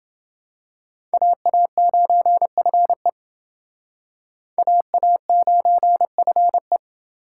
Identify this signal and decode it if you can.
Morse code, heard as a keyed tone, about 29 wpm, 700 Hz. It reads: AA9FE AA9FE